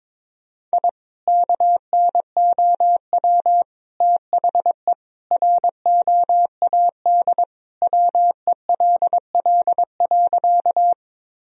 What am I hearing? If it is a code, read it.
Morse code, heard as a keyed tone, about 22 wpm, 700 Hz. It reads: I KNOW THE ROAD WELL.
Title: Sentences from Top 1000 Words - Review Part 2 - 22wpm